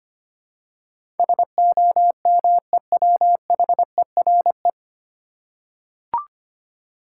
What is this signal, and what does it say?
Morse code, heard as a keyed tone, about 25 wpm, 700 Hz. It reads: SOMEWHERE
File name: practice / 25wpm / 9 Letter Words - First 3 Letters Given - 25wpm